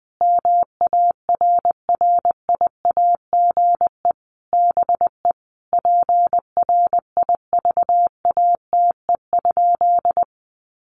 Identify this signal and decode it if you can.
Morse code, heard as a keyed tone, about 20 wpm, 700 Hz. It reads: MARRIAGE BE PRIVATE?
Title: Sentences from Top 1000 Words 20wpm